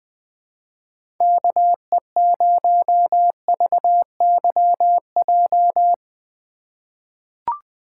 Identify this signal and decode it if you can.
Morse code, heard as a keyed tone, about 20 wpm, 700 Hz. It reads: KE0VYJ
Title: USA Call Signs - 4x Character Spacing - 20wpm